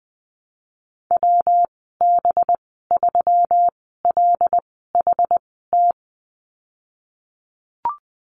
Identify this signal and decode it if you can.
Morse code, heard as a keyed tone, about 20 wpm, spaced as for 15 wpm, 700 Hz. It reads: WB3LHT
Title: 213 Lesson 34g - Number 8 - Call Signs 20wpm